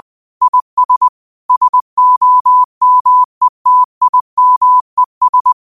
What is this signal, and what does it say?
Morse code, heard as a keyed tone, about 20 wpm, 1000 Hz. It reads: IS SOMETIMES
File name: qotdfiles / 20wpm